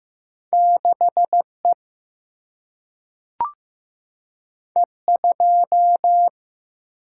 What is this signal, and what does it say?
Morse code, heard as a keyed tone, about 15 wpm, 700 Hz. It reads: 6E E2
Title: Single-Letter Single-Number 15wpm